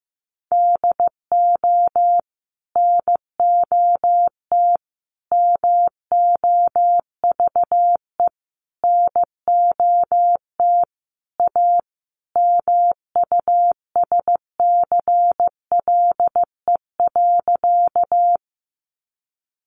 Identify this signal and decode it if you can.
Morse code, heard as a keyed tone, about 14 wpm, 700 Hz. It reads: DO NOT MOVE NOT A MUSCLE.
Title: Sentences from Top 1300 Words 15wpm